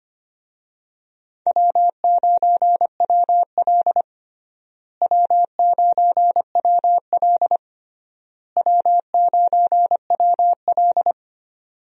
Morse code, heard as a keyed tone, about 25 wpm, 700 Hz. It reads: W9WL W9WL W9WL
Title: USA Call Signs - 4 Characters - Repeated 3x - 25wpm